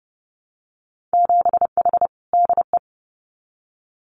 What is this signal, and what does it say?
Morse code, heard as a keyed tone, about 30 wpm, 700 Hz. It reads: 7H DE
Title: Winter Field Day Exchanges Encore 30wpm